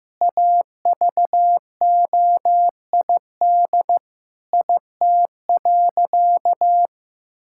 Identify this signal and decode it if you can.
Morse code, heard as a keyed tone, about 15 wpm, 700 Hz. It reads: AVOID IT.
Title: Sentences from Top 1000 Words - Review Part 6 - 15wpm